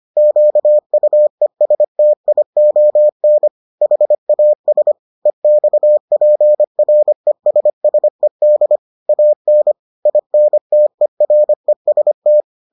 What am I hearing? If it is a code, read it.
Morse code, heard as a keyed tone, about 25 wpm, 600 Hz. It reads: QUESTION HAS EXPRESSED AN INTEREST